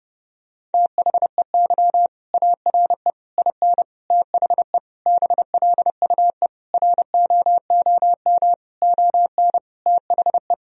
Morse code, heard as a keyed tone, about 30 wpm, 700 Hz. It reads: THEY ARE IN THE BLUE ROOM ON THE
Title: Sentences from Top 1000 Words - Review Part 3 - 30wpm